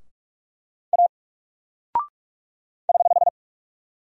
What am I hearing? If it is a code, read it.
Morse code, heard as a keyed tone, about 44 wpm, 700 Hz. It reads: A <HH>
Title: Single Character (Letter:Number:Punctuation:Prosign) 45wpm